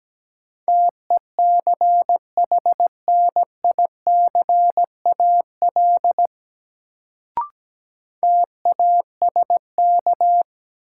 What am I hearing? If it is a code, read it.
Morse code, heard as a keyed tone, about 17 wpm, 700 Hz. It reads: TECHNICAL TASK
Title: Sentences from Top 1100 Words - New Word Drill Alt - 17wpm